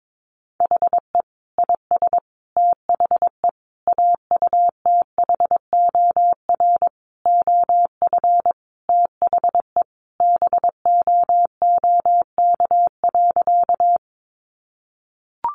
Morse code, heard as a keyed tone, about 22 wpm, 700 Hz. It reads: HE IS THE AUTHOR OF THE BOOK.
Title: Sentences from Top 1200 Words 22wpm